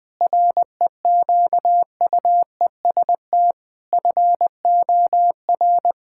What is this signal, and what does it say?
Morse code, heard as a keyed tone, about 20 wpm, 700 Hz. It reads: REQUEST FOR
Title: Sets of 8 Words - 20wpm